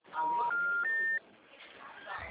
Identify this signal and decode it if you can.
Special information tone, with speech.